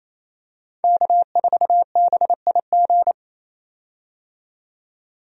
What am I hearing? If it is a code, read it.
Morse code, heard as a keyed tone, about 28 wpm, 700 Hz. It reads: K4BIG
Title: USA Call Signs - 5 Characters - 28wpm